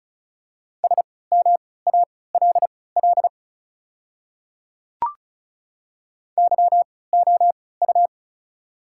Morse code, heard as a keyed tone, about 34 wpm, spaced as for 20 wpm, 700 Hz. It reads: SMALL YOU
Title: 3x Character Spacing - Top 200 Words - 35wpm